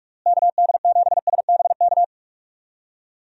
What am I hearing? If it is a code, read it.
Morse code, heard as a keyed tone, about 42 wpm, 700 Hz. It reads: KD7SBK
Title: USA Call Signs - 6 Characters - Repeated 3x - 45wpm